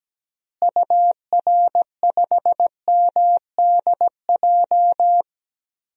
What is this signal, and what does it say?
Morse code, heard as a keyed tone, about 17 wpm, 700 Hz. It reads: UR5MDJ
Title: DXCC-100 Call Signs - 6 Characters - Repeated 3x - 17wpm